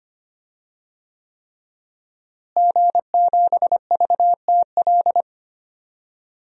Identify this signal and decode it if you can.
Morse code, heard as a keyed tone, about 25 wpm, 700 Hz. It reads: G7VTL
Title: DXCC-100 Call Signs - 5 Characters - Repeated 3x - 25wpm